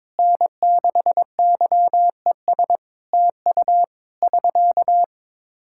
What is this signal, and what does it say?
Morse code, heard as a keyed tone, about 21 wpm, 700 Hz. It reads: N6YES TU <SK>
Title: Practice QSO - Exchange 4 - Ending - 22wpm